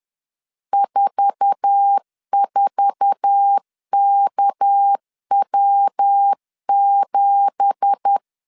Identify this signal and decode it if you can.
Morse code, heard as a keyed tone, about 10 wpm, 800 Hz. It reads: 44KW7